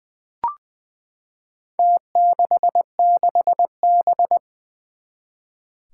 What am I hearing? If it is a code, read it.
Morse code, heard as a keyed tone, about 20 wpm, 700 Hz. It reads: T66B
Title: Four Challenging Letter-Number Combinations 20wpm